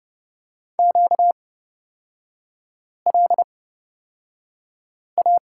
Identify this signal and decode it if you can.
Morse code, heard as a keyed tone, about 29 wpm, spaced as for 5 wpm, 700 Hz. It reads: QLA
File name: Lesson 38e - Letter q - Three Characters 30wpm